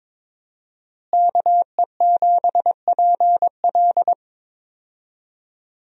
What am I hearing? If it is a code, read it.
Morse code, heard as a keyed tone, about 21 wpm, 700 Hz. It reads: KE7PL
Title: USA Call Signs - 5 Characters - Repeated 2x - 22wpm